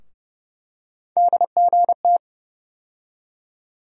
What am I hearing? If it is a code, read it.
Morse code, heard as a keyed tone, about 30 wpm, 700 Hz. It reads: DGT
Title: USA Call Sign Postfixes - 3 Letters 30wpm